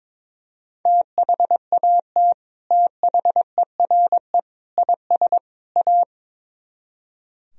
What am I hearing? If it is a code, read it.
Morse code, heard as a keyed tone, about 22 wpm, 700 Hz. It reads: THAT THERE IS A
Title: Speed-Racing - Sets of 4 Words 22wpm